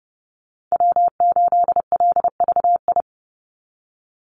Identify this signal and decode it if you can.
Morse code, heard as a keyed tone, about 30 wpm, 700 Hz. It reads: W8LVI
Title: Call Signs Repeated 2x - 30wpm